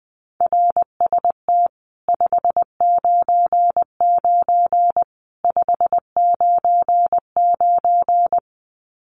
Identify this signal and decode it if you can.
Morse code, heard as a keyed tone, about 20 wpm, 700 Hz. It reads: RST 599 599
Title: Common QSO Phrases 20wpm